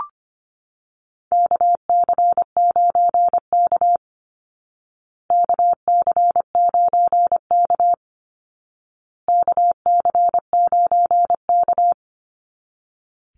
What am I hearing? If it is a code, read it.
Morse code, heard as a keyed tone, about 24 wpm, 700 Hz. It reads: KC9K KC9K KC9K
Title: Call Signs Repeated 2x - 25wpm